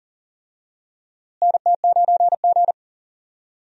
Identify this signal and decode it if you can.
Morse code, heard as a keyed tone, about 40 wpm, 700 Hz. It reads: NT9G
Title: USA Call Signs - 4 Characters - Repeated 2x - 40wpm